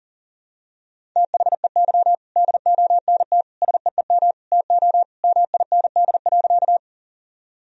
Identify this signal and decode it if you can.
Morse code, heard as a keyed tone, about 38 wpm, 700 Hz. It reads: THEY DONT SEEM TO MIND.
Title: Sentences from Top 1000 Words - Review Part 1 - 40wpm